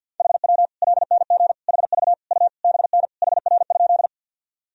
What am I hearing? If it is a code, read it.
Morse code, heard as a keyed tone, about 52 wpm, 700 Hz. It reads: HW LNG HV U BN HR?
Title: Sentences from Top 200 Words and Abbreviations 50wpm